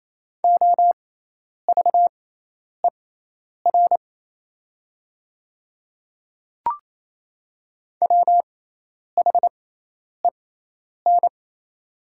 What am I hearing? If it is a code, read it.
Morse code, heard as a keyed tone, about 28 wpm, spaced as for 10 wpm, 690 Hz. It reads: OVER WHEN
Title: 6x Character Spacing - Top 100 Words - 28wpm